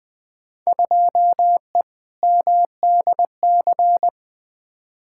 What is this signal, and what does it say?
Morse code, heard as a keyed tone, about 21 wpm, 700 Hz. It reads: 2E MDC
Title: Field Day Exchanges - 20wpm